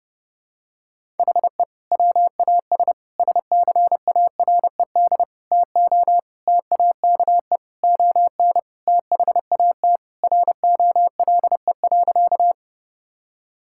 Morse code, heard as a keyed tone, about 30 wpm, 700 Hz. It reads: HE WAS SCARED TO TAKE ON THAT ROLE.
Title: Sentences from Top 1000 Words - Review Part 6 - 30wpm